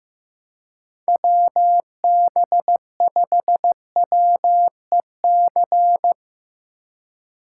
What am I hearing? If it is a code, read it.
Morse code, heard as a keyed tone, about 15 wpm, 700 Hz. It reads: WB5WEC
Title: K1USN SST QSO Practice - 15wpm